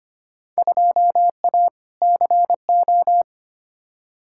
Morse code, heard as a keyed tone, about 25 wpm, 700 Hz. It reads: 2A CO
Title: Field Day Exchanges - 25wpm